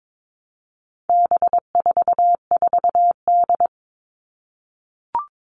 Morse code, heard as a keyed tone, about 22 wpm, 700 Hz. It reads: B44D
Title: All Cut Numbers - 4-Digit 22wpm